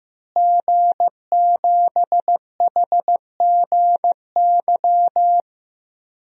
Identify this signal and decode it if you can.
Morse code, heard as a keyed tone, about 15 wpm, 700 Hz. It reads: G7HGY
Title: DXCC-100 Call Signs - 5 Characters - Repeated 2x - 15wpm